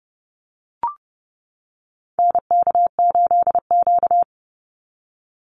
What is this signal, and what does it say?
Morse code, heard as a keyed tone, about 30 wpm, 700 Hz. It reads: NK8Q
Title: NA SOTA Activator Callsigns 30wpm